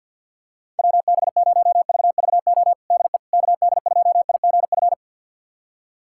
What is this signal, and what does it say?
Morse code, heard as a keyed tone, about 46 wpm, 700 Hz. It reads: WB0VVO DE KD2IGF
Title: USA Callsign DE Callsign - 6 Characters - 50wpm